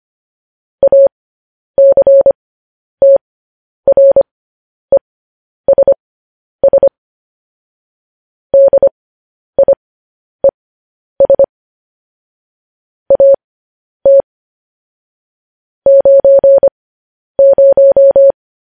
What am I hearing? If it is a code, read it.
Morse code, heard as a keyed tone, about 25 wpm, spaced as for 10 wpm, 550 Hz. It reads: ACTRESS DIES AT 90
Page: news headlines in morse code at 10 wpm